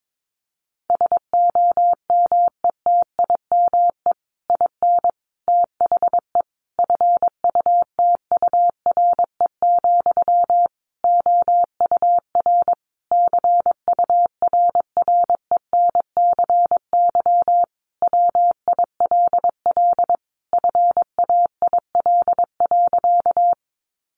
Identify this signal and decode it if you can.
Morse code, heard as a keyed tone, about 22 wpm, 700 Hz. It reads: SOMETIME IN THE FUTURE, OUR CURRENCY WILL FAIL.
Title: Sentences from Top 1500 Words 22wpm